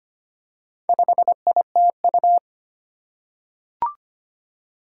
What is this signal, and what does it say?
Morse code, heard as a keyed tone, about 25 wpm, 700 Hz. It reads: 5ITU
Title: Four Challenging Letter-Number Combinations 25wpm